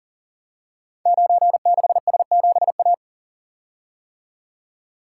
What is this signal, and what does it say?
Morse code, heard as a keyed tone, about 39 wpm, 690 Hz. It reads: 96S7A
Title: Code Groups - 5 Letter:Numbers - 40wpm